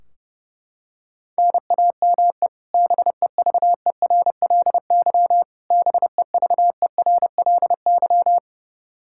Morse code, heard as a keyed tone, about 32 wpm, 700 Hz. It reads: NAME BEVERLY BEVERLY
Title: Practice QSO - Exchange 1 - Beginning - 30wpm